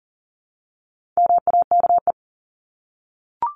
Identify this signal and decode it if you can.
Morse code, heard as a keyed tone, about 40 wpm, 700 Hz. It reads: MAKE
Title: Top 500 Words 40wpm